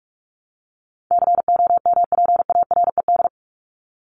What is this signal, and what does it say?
Morse code, heard as a keyed tone, about 44 wpm, 700 Hz. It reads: COMPARED
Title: ED Suffix - Top 100 Words with Base Words 45wpm